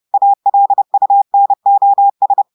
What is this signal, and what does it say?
Morse code, heard as a keyed tone, about 30 wpm, 800 Hz. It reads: ALUNOS
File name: audios / portugais / portugais texte en clair 1 vitesse 30 mots minutes